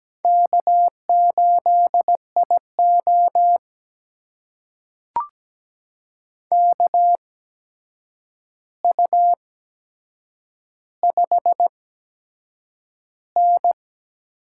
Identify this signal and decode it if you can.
Morse code, heard as a keyed tone, about 17 wpm, spaced as for 7 wpm, 700 Hz. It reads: K8IO KU5N
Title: USA Call Signs - 4 Characters - 8x Character Spacing - 17wpm